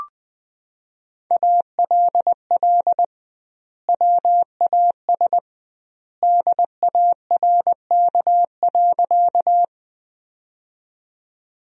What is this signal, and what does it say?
Morse code, heard as a keyed tone, about 21 wpm, 700 Hz. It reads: ALL WAS DARK.
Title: 2x Word-Spacing - Sentences from Top 500 Words - 20wpm